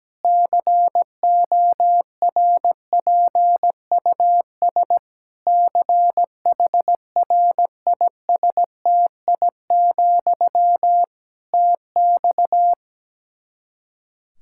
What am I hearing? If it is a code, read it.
Morse code, heard as a keyed tone, about 17 wpm, 700 Hz. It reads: CORPUS CHRISTI, TX
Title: Practice QSO - Exchange 1 - Beginning - 17wpm